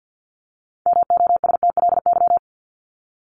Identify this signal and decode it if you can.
Morse code, heard as a keyed tone, about 50 wpm, 700 Hz. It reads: MOSTLY
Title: LY Suffix - Top 100 Words with Base Words 50wpm